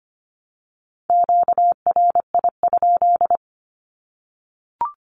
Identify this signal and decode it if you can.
Morse code, heard as a keyed tone, about 25 wpm, 700 Hz. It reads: QRI?
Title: Q-Codes Repeated 3x 25wpm